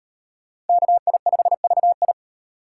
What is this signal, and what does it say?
Morse code, heard as a keyed tone, about 38 wpm, 700 Hz. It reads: KI5VI